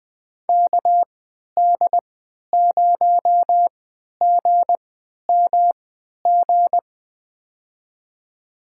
Morse code, heard as a keyed tone, about 20 wpm, spaced as for 11 wpm, 700 Hz. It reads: KD0GMG
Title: USA Call Signs - 6 Characters - 3x Character Spacing - 20wpm